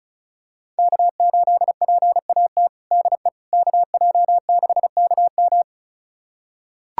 Morse code, heard as a keyed tone, about 34 wpm, 700 Hz. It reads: K8PAT DE KJ6KM
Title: USA Callsign DE Callsign - 5 Characters - 35wpm